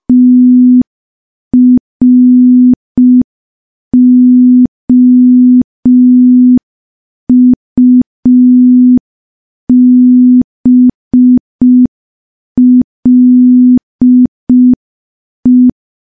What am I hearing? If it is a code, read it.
Morse code, heard as a keyed tone, about 5 wpm, 250 Hz. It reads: TROUBLE
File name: morse / morse-training-05WPM